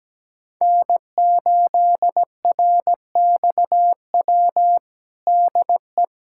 Morse code, heard as a keyed tone, about 17 wpm, 700 Hz. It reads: N8RXW DE
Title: USA Callsign DE Callsign - 5 Characters - 17wpm